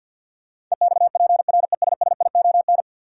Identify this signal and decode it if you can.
Morse code, heard as a keyed tone, about 52 wpm, 700 Hz. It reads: EXPRESSION
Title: Sentences from Top 1200 Words - New Word Drill - 50wpm